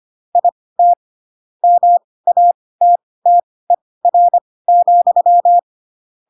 Morse code, heard as a keyed tone, about 25 wpm, spaced as for 18 wpm, 700 Hz. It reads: IT MATTER,